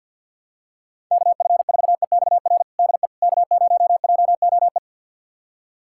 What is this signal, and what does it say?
Morse code, heard as a keyed tone, about 50 wpm, 700 Hz. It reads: KF4EXR DE K0JOE
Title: USA Callsign DE Callsign 50wpm